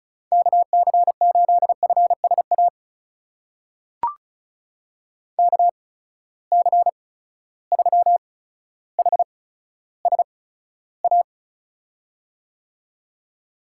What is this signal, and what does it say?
Morse code, heard as a keyed tone, about 36 wpm, spaced as for 15 wpm, 700 Hz. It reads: KC8FSA KC3HSA
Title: USA Call Signs - 6 Characters - 8x Character Spacing - 35wpm